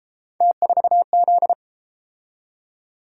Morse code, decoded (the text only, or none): T4Z